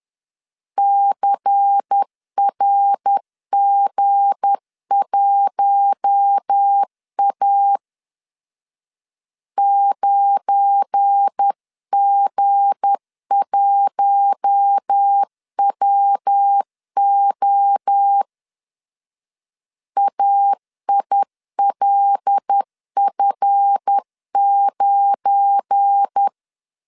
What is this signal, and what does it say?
Morse code, heard as a keyed tone, about 10 wpm, 800 Hz. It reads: CRG1A 9G1WO AILF9
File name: LEZIONI / 7 GF9Z / 7 GF9 10wpm 2747